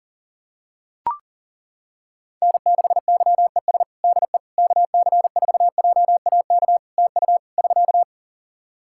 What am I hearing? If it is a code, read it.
Morse code, heard as a keyed tone, about 40 wpm, 700 Hz. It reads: N6YES DE KC4JAK TU <SK>
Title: K1USN SST QSO Practice 40wpm